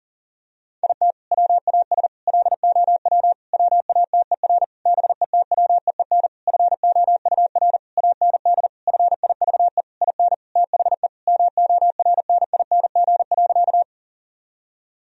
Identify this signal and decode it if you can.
Morse code, heard as a keyed tone, about 40 wpm, 700 Hz. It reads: IT WAS LOW WATER BETWEEN FOUR AND FIVE IN THE MORNING.
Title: Sentences from Top 1000 Words - Review Part 2 - 40wpm